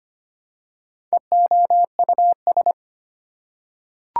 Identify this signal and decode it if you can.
Morse code, heard as a keyed tone, about 25 wpm, 700 Hz. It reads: EOUS
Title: Prefixes and Suffixes 25wpm